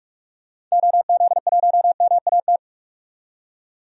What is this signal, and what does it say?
Morse code, heard as a keyed tone, about 44 wpm, 700 Hz. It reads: OZ1MAT
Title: DXCC-100 Call Signs - 6 Characters - 45wpm